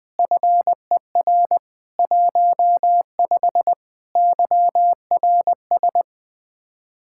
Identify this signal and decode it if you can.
Morse code, heard as a keyed tone, about 20 wpm, 700 Hz. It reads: FER 15 YRS
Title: Speed-Racing Common QSO Phrases Encore - 20wpm